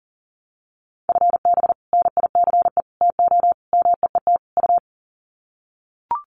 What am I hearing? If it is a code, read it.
Morse code, heard as a keyed tone, about 40 wpm, 700 Hz. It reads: FB NICE TO MEET U
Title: Common QSO Phrases 40wpm